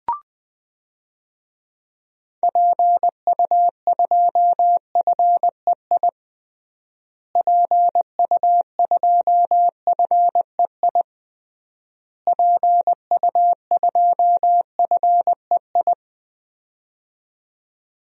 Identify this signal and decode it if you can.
Morse code, heard as a keyed tone, about 20 wpm, 700 Hz. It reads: PU2FEI PU2FEI PU2FEI
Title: DXCC-100 Call Signs - 6 Characters - Repeated 3x - 20wpm